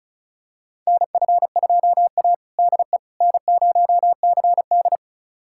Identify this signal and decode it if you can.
Morse code, heard as a keyed tone, about 34 wpm, 700 Hz. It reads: NF2A DE N0CD
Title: USA Callsign DE Callsign - 4 Characters - 35wpm